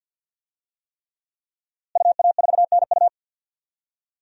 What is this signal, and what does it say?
Morse code, heard as a keyed tone, about 50 wpm, 700 Hz. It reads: UA4NU
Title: DXCC-100 Call Signs - 5 Characters - Repeated 3x - 50wpm